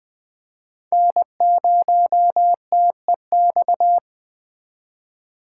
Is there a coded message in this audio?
Yes: Morse code from 0.9 to 4.0 s, heard as a keyed tone, about 22 wpm, 700 Hz. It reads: N0TEX